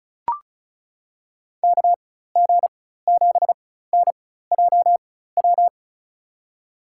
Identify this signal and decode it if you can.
Morse code, heard as a keyed tone, about 34 wpm, spaced as for 16 wpm, 700 Hz. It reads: KG7NJW
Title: USA Call Signs - 4x Character Spacing - 35wpm